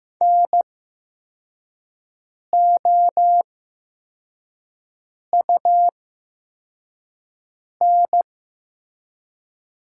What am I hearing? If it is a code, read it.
Morse code, heard as a keyed tone, about 15 wpm, spaced as for 4 wpm, 700 Hz. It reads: NOUN
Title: 8x Character Spacing - Top 500 Words - 15wpm